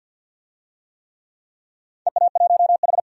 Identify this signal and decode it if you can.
Morse code, heard as a keyed tone, about 52 wpm, 700 Hz. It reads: EA1H
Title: DXCC-100 Call Signs - 4 Characters - 50wpm